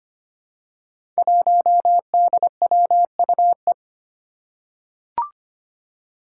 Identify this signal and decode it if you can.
Morse code, heard as a keyed tone, about 25 wpm, 700 Hz. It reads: 1DWUE
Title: Five Challenging Letter-Number Combinations 25wpm